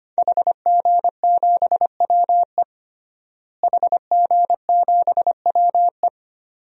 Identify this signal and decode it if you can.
Morse code, heard as a keyed tone, about 25 wpm, 700 Hz. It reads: HG7WE HG7WE
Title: DXCC-100 Call Signs - 5 Characters - Repeated 2x - 25wpm